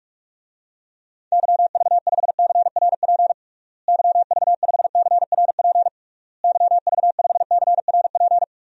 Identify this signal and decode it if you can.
Morse code, heard as a keyed tone, about 41 wpm, 700 Hz. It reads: YV5CRP YV5CRP YV5CRP
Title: DXCC-100 Call Signs - 6 Characters - Repeated 3x - 45wpm